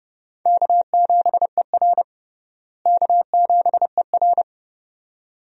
Morse code, heard as a keyed tone, about 30 wpm, 700 Hz. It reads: K7ER K7ER
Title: USA Call Signs - 4 Characters - Repeated 2x - 30wpm